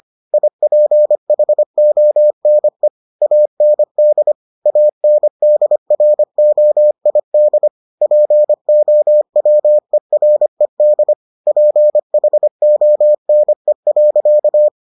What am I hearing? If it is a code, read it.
Morse code, heard as a keyed tone, about 25 wpm, 600 Hz. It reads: IPHONE AND ANDROID POWERED PHONE.